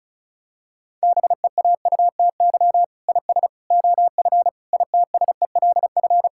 Morse code, heard as a keyed tone, about 34 wpm, 700 Hz. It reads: BEAUTY IS OF ITSELF
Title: Sentences from Top 1000 Words - Review Part 5 - 35wpm